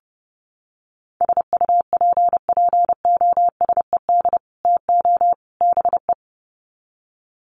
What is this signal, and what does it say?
Morse code, heard as a keyed tone, about 30 wpm, 700 Hz. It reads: SUPPOSED TO BE
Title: Sets of 3 Words 30wpm